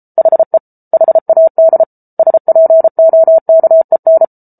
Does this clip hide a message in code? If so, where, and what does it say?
Yes: Morse code from 0.2 to 4.2 s, heard as a keyed tone, about 34 wpm, 650 Hz. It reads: HE HAD SPOKEN